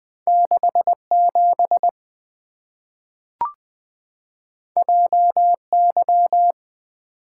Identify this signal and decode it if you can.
Morse code, heard as a keyed tone, about 20 wpm, 690 Hz. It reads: 67 JY
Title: Two Letter-Number Combinations - 20wpm